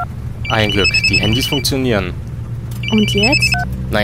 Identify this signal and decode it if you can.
Speech. There is no code.